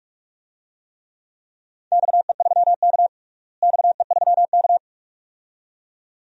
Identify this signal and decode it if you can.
Morse code, heard as a keyed tone, about 46 wpm, 700 Hz. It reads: XE3K XE3K
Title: DXCC-100 Call Signs - 4 Characters - Repeated 2x - 45wpm